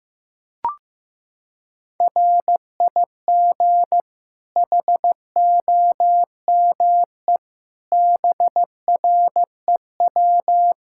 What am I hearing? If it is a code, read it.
Morse code, heard as a keyed tone, about 15 wpm, 700 Hz. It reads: RIG HOME BREW
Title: K1USN SST QSO Practice 15wpm